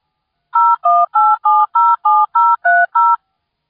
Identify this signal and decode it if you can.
DTMF, heard as telephone touch tones, tones of about 220 ms. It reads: 018707030